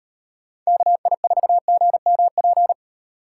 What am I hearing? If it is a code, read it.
Morse code, heard as a keyed tone, about 38 wpm, 700 Hz. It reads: KI4GMP